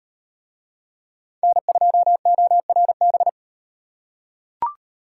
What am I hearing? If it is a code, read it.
Morse code, heard as a keyed tone, about 38 wpm, 700 Hz. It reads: N2ORB